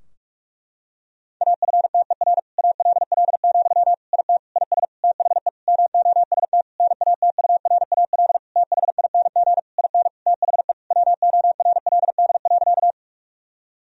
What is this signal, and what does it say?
Morse code, heard as a keyed tone, about 46 wpm, 700 Hz. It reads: AFTER ALL, IT IS THE MOST NATURAL THING IN THE WORLD.
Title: Sentences from Top 1000 Words - Review Part 4 - 45wpm